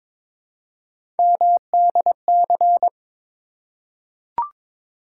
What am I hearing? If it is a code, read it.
Morse code, heard as a keyed tone, about 22 wpm, 700 Hz. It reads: MDC
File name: ARRL Sections v2 - 22wpm